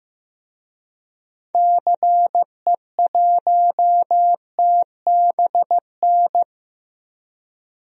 Morse code, heard as a keyed tone, about 16 wpm, 700 Hz. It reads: CE1TBN
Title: DXCC-100 Call Signs - 6 Characters - Repeated 3x - 15wpm